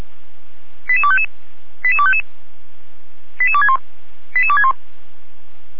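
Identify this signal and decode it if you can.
Five-tone selective call (ZVEI), heard as quick runs of five tones, tones of about 70 ms. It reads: 88255 88255 88261 88261
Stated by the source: ZVEI five-tone selective call, address 88231, received on an SDR.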